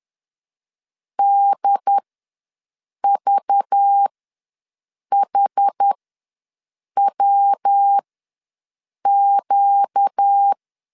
Morse code, heard as a keyed tone, about 10 wpm, spaced as for 6 wpm, 800 Hz. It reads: DVHWQ
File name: LEZIONI / 5 LBVC / 5 LBV ws 10wpm 3645